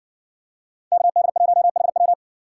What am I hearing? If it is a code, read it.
Morse code, heard as a keyed tone, about 60 wpm, 700 Hz. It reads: KD1HW